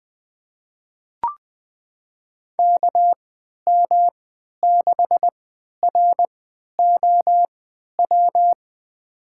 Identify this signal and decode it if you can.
Morse code, heard as a keyed tone, about 20 wpm, spaced as for 11 wpm, 700 Hz. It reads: KM6ROW
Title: USA Call Signs - 6 Characters - 3x Character Spacing - 20wpm